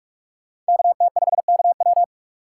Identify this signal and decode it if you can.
Morse code, heard as a keyed tone, about 45 wpm, 700 Hz. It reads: KT5KW